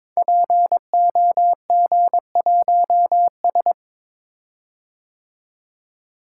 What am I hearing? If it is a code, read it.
Morse code, heard as a keyed tone, about 22 wpm, 700 Hz. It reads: POG1S
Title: Code Groups - 5 Letter:Numbers - 22wpm